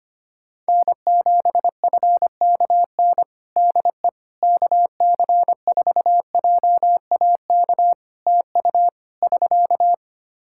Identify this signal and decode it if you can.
Morse code, heard as a keyed tone, about 25 wpm, 700 Hz. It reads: N7FKN DE KC4JAK TU <SK>
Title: Practice QSO - Exchange 4 - Ending - 25wpm